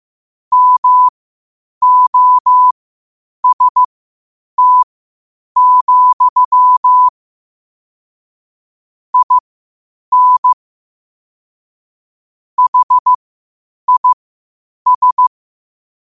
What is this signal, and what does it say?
Morse code, heard as a keyed tone, about 15 wpm, spaced as for 8 wpm, 1000 Hz. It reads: MOST, IN HIS
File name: qotdfiles / 8wpm